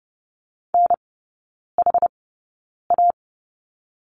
Morse code, heard as a keyed tone, about 30 wpm, spaced as for 9 wpm, 690 Hz. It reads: NHA
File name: Lesson 17e - Letter u - Three Characters 30wpm